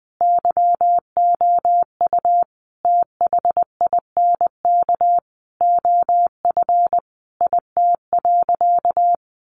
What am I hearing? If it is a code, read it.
Morse code, heard as a keyed tone, about 20 wpm, 700 Hz. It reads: YOU THINK OF IT.
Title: Sentences from Top 200 Words 20wpm